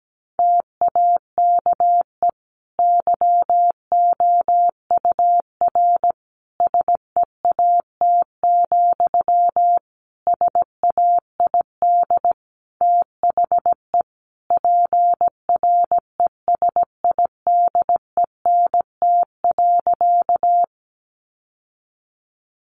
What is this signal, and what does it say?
Morse code, heard as a keyed tone, about 17 wpm, 700 Hz. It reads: TAKE YOUR SEAT, SAID THE PRESIDENT.
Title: Sentences from Top 1500 Words 17wpm